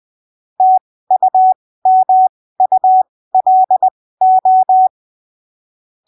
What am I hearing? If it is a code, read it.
Morse code, heard as a keyed tone, about 20 wpm, spaced as for 15 wpm, 750 Hz. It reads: TUMULO
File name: arquivos / texto / texto 27 20ppm